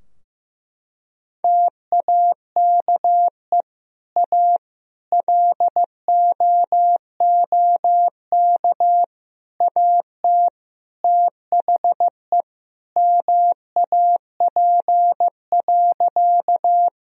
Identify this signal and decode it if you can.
Morse code, heard as a keyed tone, about 15 wpm, 700 Hz. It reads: TAKE A LOOK AT THE MAP.
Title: Sentences from Top 1000 Words - Review Part 2 - 15wpm